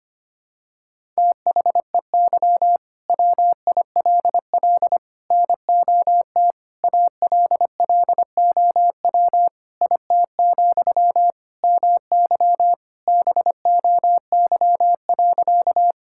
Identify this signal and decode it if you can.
Morse code, heard as a keyed tone, about 25 wpm, 700 Hz. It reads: THEY WILL NOT ALLOW IT, MY BOY.
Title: Sentences from Top 1000 Words - Review Part 5 - 25wpm